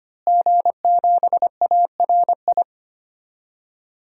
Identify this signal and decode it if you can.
Morse code, heard as a keyed tone, about 25 wpm, 700 Hz. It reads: G7ARI
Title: DXCC-100 Call Signs - 5 Characters - Repeated 3x - 25wpm